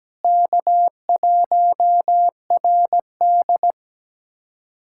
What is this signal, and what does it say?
Morse code, heard as a keyed tone, about 17 wpm, 700 Hz. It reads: K1RD
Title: USA Callsign DE Callsign - 4 Characters - 17wpm